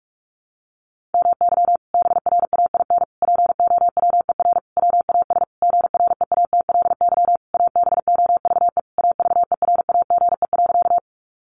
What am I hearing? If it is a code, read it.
Morse code, heard as a keyed tone, about 44 wpm, 700 Hz. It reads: MY BRAIN POWER WAS GREATLY ABOVE AVERAGE.
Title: Sentences from Top 1400 Words 45wpm